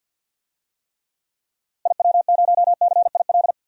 Morse code, heard as a keyed tone, about 52 wpm, 700 Hz. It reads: IW0CIL
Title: DXCC-100 Call Signs - 6 Characters - Repeated 2x - 50wpm